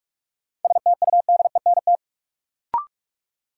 Morse code, heard as a keyed tone, about 46 wpm, 700 Hz. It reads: STUDENT
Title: Sentences from Top 1000 Words - New Word Drill - 45wpm